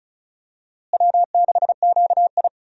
Morse code, heard as a keyed tone, about 34 wpm, 700 Hz. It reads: W6QI
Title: USA Call Signs - 4 Characters - Repeated 3x - 35wpm